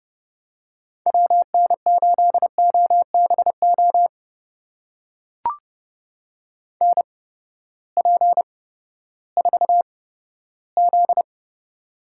Morse code, heard as a keyed tone, about 32 wpm, spaced as for 15 wpm, 700 Hz. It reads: WN8OBO NP4Z
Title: USA Call Signs - 8x Character Spacing - 30wpm